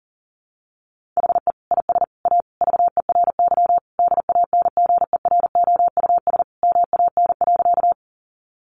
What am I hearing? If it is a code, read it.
Morse code, heard as a keyed tone, about 40 wpm, 700 Hz. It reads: HE IS A VERY DANGEROUS MAN.
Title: Sentences from Top 1200 Words 40wpm